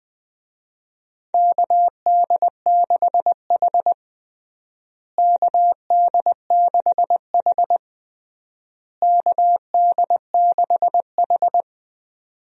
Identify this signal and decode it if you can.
Morse code, heard as a keyed tone, about 20 wpm, 700 Hz. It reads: KD6H KD6H KD6H
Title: USA Call Signs - 4 Characters - Repeated 3x - 20wpm